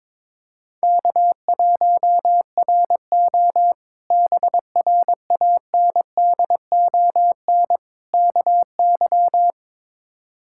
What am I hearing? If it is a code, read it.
Morse code, heard as a keyed tone, about 22 wpm, 700 Hz. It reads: K1RO BRANDON KY
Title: K1USN SST QSO Practice - 22wpm